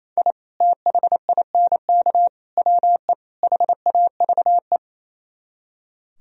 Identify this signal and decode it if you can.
Morse code, heard as a keyed tone, about 34 wpm, 700 Hz. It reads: I THINK WE HAVE
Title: Speed-Racing - Sets of 4 Words 28wpm